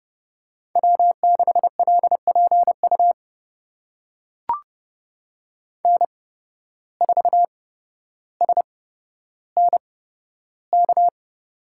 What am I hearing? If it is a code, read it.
Morse code, heard as a keyed tone, about 30 wpm, spaced as for 13 wpm, 700 Hz. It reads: W6LPU N4SNK
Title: USA Call Signs - 5 Characters - 8x Character Spacing - 30wpm